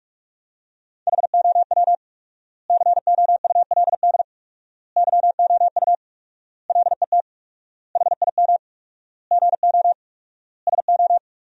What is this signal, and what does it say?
Morse code, heard as a keyed tone, about 47 wpm, 700 Hz. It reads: HOW COULD YOU LET HIM GO SO
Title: 4x Word-Spacing - Sentences from Top 300 Words - 45wpm